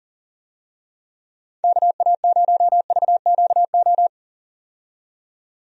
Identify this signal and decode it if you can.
Morse code, heard as a keyed tone, about 41 wpm, 700 Hz. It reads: KA0VQO
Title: USA Call Signs - 6 Characters - 40wpm